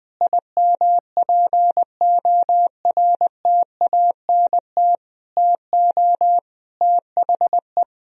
Morse code, heard as a keyed tone, about 20 wpm, 700 Hz. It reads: IMPORTANT TO THE
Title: Sentences from Top 1000 Words - Review Part 4 - 20wpm